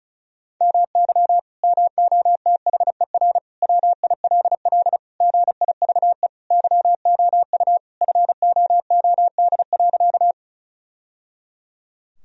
Morse code, heard as a keyed tone, about 34 wpm, 700 Hz. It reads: MY MOTHER WILL GIVE YOU FOOD.
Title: Sentences from Top 1000 Words - Review Part 1 - 35wpm